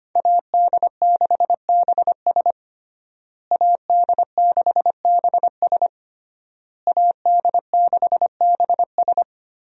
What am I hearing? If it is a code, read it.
Morse code, heard as a keyed tone, about 25 wpm, 700 Hz. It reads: AD6BS AD6BS AD6BS
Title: USA Call Signs - 5 Characters - Repeated 3x - 25wpm